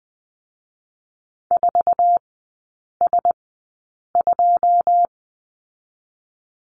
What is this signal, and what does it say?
Morse code, heard as a keyed tone, about 20 wpm, spaced as for 8 wpm, 700 Hz. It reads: 4S2
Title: 081 Lesson 14e - Number 2 - Three Characters 20wpm